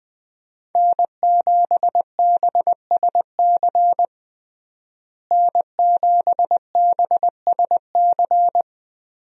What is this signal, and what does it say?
Morse code, heard as a keyed tone, about 20 wpm, 700 Hz. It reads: N7BSC N7BSC
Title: USA Call Signs - 5 Characters - Repeated 2x - 20wpm